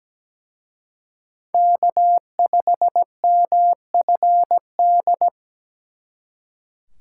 Morse code, heard as a keyed tone, about 20 wpm, 700 Hz. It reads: K5MFD
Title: Speed Racing - USA 1x3 Call Signs 17wpm